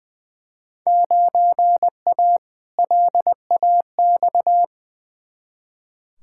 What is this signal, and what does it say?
Morse code, heard as a keyed tone, about 20 wpm, 700 Hz. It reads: 9A LAX
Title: Field Day Exchanges Encore - 20wpm